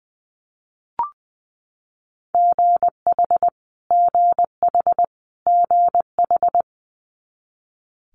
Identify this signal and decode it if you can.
Morse code, heard as a keyed tone, about 21 wpm, 700 Hz. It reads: GH GH GH
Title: Multi-letter Phonemes Repeated 3x 20wpm